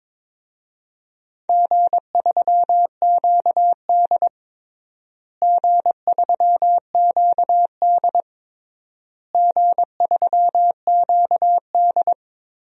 Morse code, heard as a keyed tone, about 22 wpm, 700 Hz. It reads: G3QD G3QD G3QD
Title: DXCC-100 Call Signs - 4 Characters - Repeated 3x - 22wpm